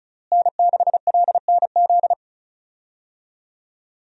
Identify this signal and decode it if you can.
Morse code, heard as a keyed tone, about 34 wpm, 700 Hz. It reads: N6LNZ
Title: USA Call Signs - 5 Characters - Repeated 2x - 35wpm